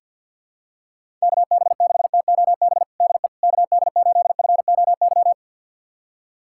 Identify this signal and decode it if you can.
Morse code, heard as a keyed tone, about 50 wpm, 700 Hz. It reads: KB6TOB DE KD8FOY
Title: USA Callsign DE Callsign 50wpm